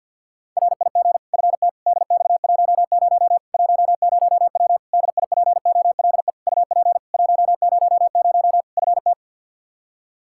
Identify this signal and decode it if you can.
Morse code, heard as a keyed tone, about 46 wpm, 700 Hz. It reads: RIG FT DX10 10W DIPOLE UP 100 FT
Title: Practice QSO - Exchange 2 - Info - 50wpm